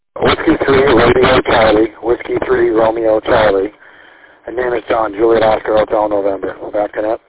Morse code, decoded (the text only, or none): none